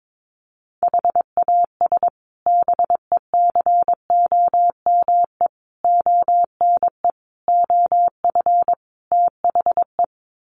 Morse code, heard as a keyed tone, about 22 wpm, 700 Hz. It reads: HAS BECOME ONE OF THE
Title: Sets of 5 Words Encore 22wpm